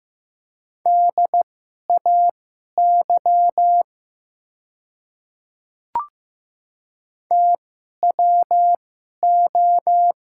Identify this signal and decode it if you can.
Morse code, heard as a keyed tone, about 15 wpm, spaced as for 11 wpm, 700 Hz. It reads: DAY TWO
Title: 2x Character Spacing - Top 100 Words - 15wpm